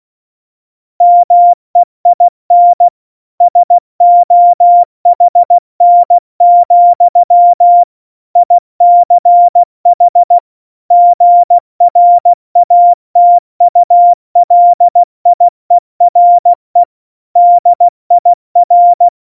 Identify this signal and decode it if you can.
Morse code, heard as a keyed tone, about 16 wpm, 700 Hz. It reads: MEIN SOHN, ICH GRATULIERE DIR
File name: Morsekurs / Witze / Witz 24. September 2019, 70 F80